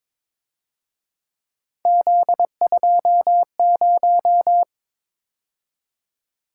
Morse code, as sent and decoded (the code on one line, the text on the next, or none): --.. ..--- -----
Z20